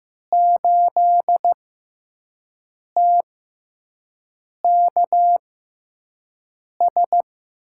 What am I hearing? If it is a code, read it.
Morse code, heard as a keyed tone, about 15 wpm, spaced as for 5 wpm, 700 Hz. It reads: 8TKS
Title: USA Call Signs - 6x Character Spacing - 15wpm